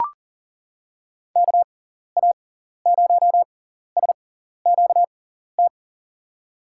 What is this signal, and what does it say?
Morse code, heard as a keyed tone, about 37 wpm, spaced as for 14 wpm, 700 Hz. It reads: KA0SQT
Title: USA Call Signs - 6 Characters - 6x Character Spacing - 40wpm